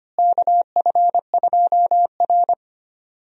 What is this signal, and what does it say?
Morse code, heard as a keyed tone, about 25 wpm, 700 Hz. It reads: KF2R
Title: USA Call Signs - 4 Characters - Repeated 2x - 25wpm